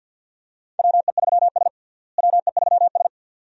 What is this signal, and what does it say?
Morse code, heard as a keyed tone, about 50 wpm, 700 Hz. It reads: WE3S WE3S
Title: USA Call Signs - 4 Characters - Repeated 2x - 50wpm